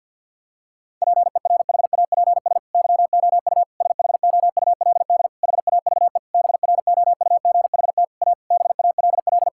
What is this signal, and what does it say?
Morse code, heard as a keyed tone, about 50 wpm, 700 Hz. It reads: PERHAPS YOU SHOULD HAVE BROUGHT A BALL
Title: Sentences from Top 1000 Words - Review Part 6 - 50wpm